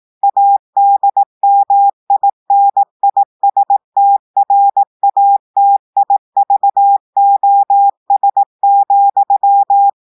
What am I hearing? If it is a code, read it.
Morse code, heard as a keyed tone, about 18 wpm, 800 Hz. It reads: ADMINISTRATIVOS,
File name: audios / portugais / portugais texte en clair 2 vitesse 18 mots minutes